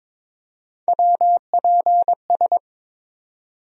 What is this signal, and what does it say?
Morse code, heard as a keyed tone, about 22 wpm, 700 Hz. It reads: WPS
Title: USA Call Sign Postfixes - 3 Letters 22wpm